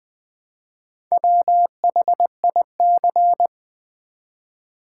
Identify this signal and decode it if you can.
Morse code, heard as a keyed tone, about 20 wpm, 700 Hz. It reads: WHIC
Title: Build A Word - 1 Letter At A Time - 5-Letter Words - 20wpm